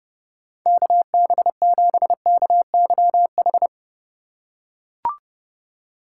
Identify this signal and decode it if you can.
Morse code, heard as a keyed tone, about 30 wpm, 700 Hz. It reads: KB7KYH
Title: USA Call Signs - 2x Character Spacing - 30wpm